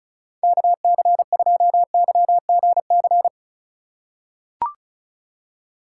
Morse code, heard as a keyed tone, about 35 wpm, 700 Hz. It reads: KC2YGC
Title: USA Call Signs - 6 Characters - 3x Character Spacing - 35wpm